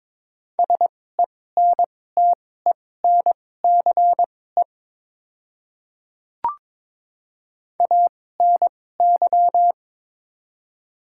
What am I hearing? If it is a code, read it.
Morse code, heard as a keyed tone, about 22 wpm, spaced as for 16 wpm, 700 Hz. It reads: SENTENCE ANY
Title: 2x Character Spacing - Top 300 Words - 22wpm